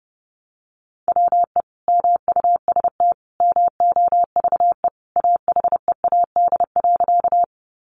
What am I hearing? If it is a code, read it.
Morse code, heard as a keyed tone, about 30 wpm, 700 Hz. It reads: WE MUST MOVE AHEAD.
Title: Sentences from Top 1100 Words 30wpm